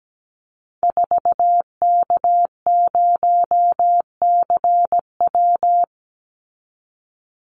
Morse code, heard as a keyed tone, about 17 wpm, 700 Hz. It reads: 4K0CW
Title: Call Signs Repeated 3x - 17wpm